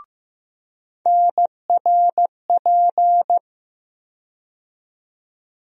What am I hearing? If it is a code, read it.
Morse code, heard as a keyed tone, about 15 wpm, 700 Hz. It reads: NRP
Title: USA Call Sign Postfixes - 3 Letters 15wpm